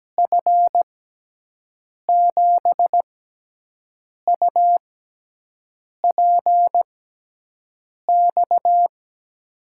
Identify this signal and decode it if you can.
Morse code, heard as a keyed tone, about 17 wpm, spaced as for 6 wpm, 700 Hz. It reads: F7UPX
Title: USA Call Signs - 6x Character Spacing - 17wpm